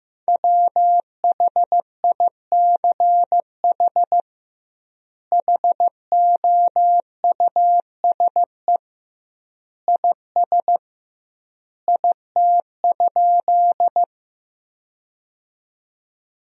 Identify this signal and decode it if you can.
Morse code, heard as a keyed tone, about 16 wpm, 700 Hz. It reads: WHICH HOUSE IS IT?
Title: 2x Word-Spacing - Sentences from Top 200 Words - 15wpm